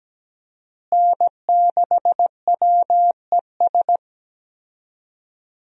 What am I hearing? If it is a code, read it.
Morse code, heard as a keyed tone, about 17 wpm, 700 Hz. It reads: N6WES
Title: USA Call Signs - 5 Characters - Repeated 3x - 17wpm